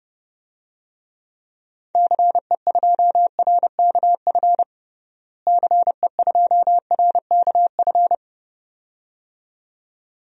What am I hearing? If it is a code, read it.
Morse code, heard as a keyed tone, about 30 wpm, 700 Hz. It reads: CE2RKF CE2RKF
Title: DXCC-100 Call Signs - 6 Characters - Repeated 2x - 30wpm